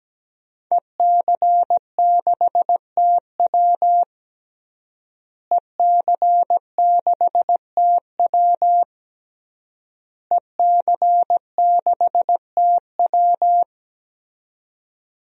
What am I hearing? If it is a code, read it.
Morse code, heard as a keyed tone, about 17 wpm, 700 Hz. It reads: EC6TW EC6TW EC6TW
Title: DXCC-100 Call Signs - 5 Characters - Repeated 3x - 17wpm